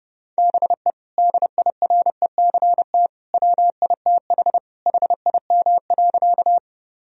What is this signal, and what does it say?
Morse code, heard as a keyed tone, about 32 wpm, 700 Hz. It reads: BE DIRECT WITH HIM.
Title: Sentences from Top 1000 Words - Review Part 2 - 30wpm